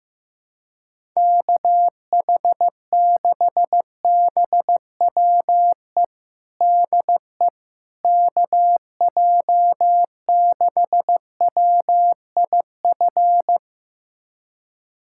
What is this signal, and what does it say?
Morse code, heard as a keyed tone, about 15 wpm, 700 Hz. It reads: KH6BWE DE KJ6WIF
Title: USA Callsign DE Callsign 15wpm